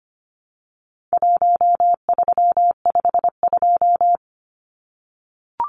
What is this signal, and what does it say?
Morse code, heard as a keyed tone, about 25 wpm, 700 Hz. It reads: 1352
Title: Numbers - Contest Style - 4-Digit 25wpm